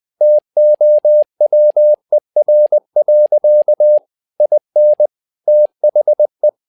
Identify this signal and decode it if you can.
Morse code, heard as a keyed tone, about 20 wpm, 600 Hz. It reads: TOWER. IN THE
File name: learning center / tower trouble / tower trouble 20wpm